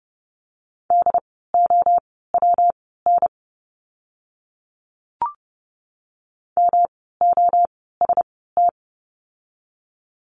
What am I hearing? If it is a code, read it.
Morse code, heard as a keyed tone, about 30 wpm, spaced as for 17 wpm, 700 Hz. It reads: DOWN MOST
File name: course / 30:10wpm / 237 Lesson 38f - Letter q - Words 30wpm